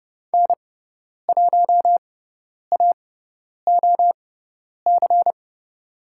Morse code, heard as a keyed tone, about 30 wpm, spaced as for 10 wpm, 700 Hz. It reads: N1AOC
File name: Lesson 18g - Letter m - Call Signs 30wpm